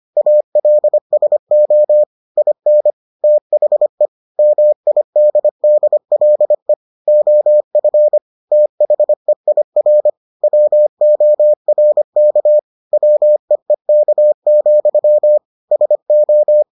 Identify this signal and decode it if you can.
Morse code, heard as a keyed tone, about 25 wpm, 600 Hz. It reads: ALSO IN THE MIDDLE OF THEIR WORK WEEK, SO